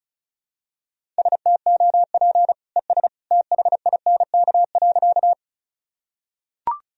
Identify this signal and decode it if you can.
Morse code, heard as a keyed tone, about 36 wpm, 700 Hz. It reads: STOP ES THINK.
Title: Sentences from Top 300 Words and Abbreviations 35wpm